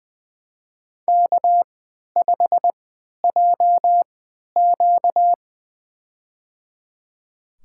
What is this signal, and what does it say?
Morse code, heard as a keyed tone, about 20 wpm, spaced as for 11 wpm, 700 Hz. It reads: K5JQ
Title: USA Call Signs - 4 Characters - 3x Character Spacing - 20wpm